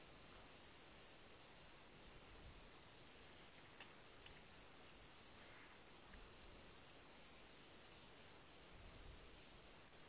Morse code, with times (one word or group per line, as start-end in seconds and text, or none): none